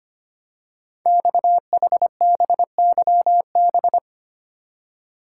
Morse code, heard as a keyed tone, about 25 wpm, 700 Hz. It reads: XHBYB
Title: Code Groups - 5 Letters - 25wpm